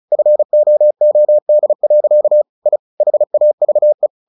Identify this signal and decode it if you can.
Morse code, heard as a keyed tone, about 34 wpm, 600 Hz. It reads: FOOD. I HAVE
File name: the old man and the sea 2 35wpm